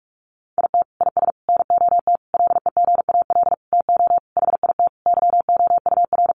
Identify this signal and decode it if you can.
Morse code, heard as a keyed tone, about 44 wpm, 700 Hz. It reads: IT IS NOT LEGAL TO HIT YOUR
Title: Sentences from Top 1200 Words 45wpm